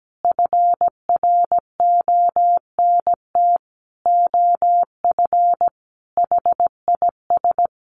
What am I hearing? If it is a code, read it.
Morse code, heard as a keyed tone, about 17 wpm, 700 Hz. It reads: FRONT OF HIS
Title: Sentences from Top 1300 Words 17wpm